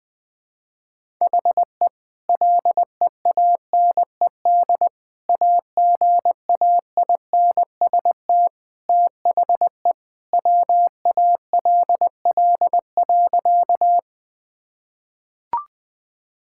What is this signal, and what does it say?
Morse code, heard as a keyed tone, about 20 wpm, 700 Hz. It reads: HE LEANED AGAINST THE WALL.
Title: Sentences with 2-Word Phrases - Spoken Beginning - 20wpm